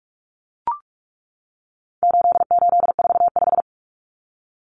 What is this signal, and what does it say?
Morse code, heard as a keyed tone, about 44 wpm, 700 Hz. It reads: 8845
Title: Numbers - 4-Digit 45wpm